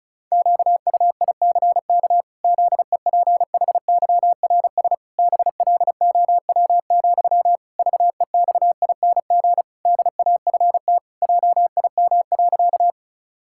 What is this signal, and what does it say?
Morse code, heard as a keyed tone, about 36 wpm, 700 Hz. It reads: QUICK ZEPHYRS BLOW, VEXING DAFT JIM.
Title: Pangram Challenge v2 - 35wpm